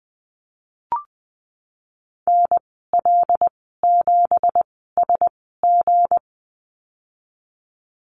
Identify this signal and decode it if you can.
Morse code, heard as a keyed tone, about 20 wpm, 700 Hz. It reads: N L 7 S G
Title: 238 Lesson 38g - Letter q - Call Signs 20wpm